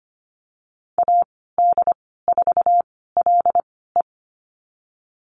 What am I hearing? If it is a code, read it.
Morse code, heard as a keyed tone, about 25 wpm, spaced as for 16 wpm, 700 Hz. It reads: AD4LE
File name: Lesson 15g - Number 5 - Call Signs 25wpm